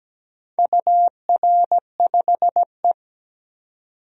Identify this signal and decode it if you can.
Morse code, heard as a keyed tone, about 17 wpm, 700 Hz. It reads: UR5E
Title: DXCC-100 Call Signs - 4 Characters - 17wpm